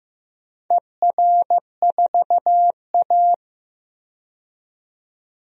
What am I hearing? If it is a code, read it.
Morse code, heard as a keyed tone, about 15 wpm, 700 Hz. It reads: ER4A
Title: DXCC-100 Call Signs - 4 Characters - 15wpm